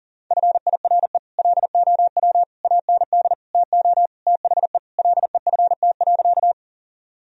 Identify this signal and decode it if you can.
Morse code, heard as a keyed tone, about 40 wpm, 700 Hz. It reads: FIRE LOW AND TO THE LEFT.
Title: Sentences from Top 1000 Words - Review Part 1 - 40wpm